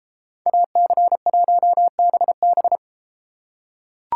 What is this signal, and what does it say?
Morse code, heard as a keyed tone, about 32 wpm, 700 Hz. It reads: AC1BB